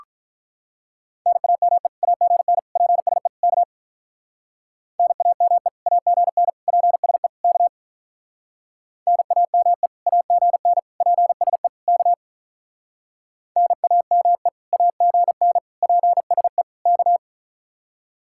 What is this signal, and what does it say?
Morse code, heard as a keyed tone, about 46 wpm, 700 Hz. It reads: NAME AGN PSE K NAME AGN PSE K NAME AGN PSE K NAME AGN PSE K